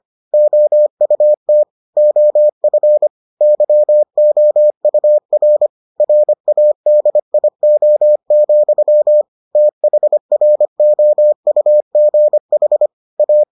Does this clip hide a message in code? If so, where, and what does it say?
Yes: Morse code from 0.3 to 13.4 s, heard as a keyed tone, about 25 wpm, 600 Hz. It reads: OUT OF YOUR RADIO, THROUGH A